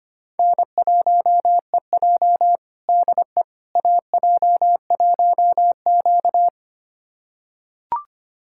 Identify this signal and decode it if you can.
Morse code, heard as a keyed tone, about 25 wpm, 700 Hz. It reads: N1EJ DE AJ1Q
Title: USA Callsign DE Callsign - 4 Characters - 25wpm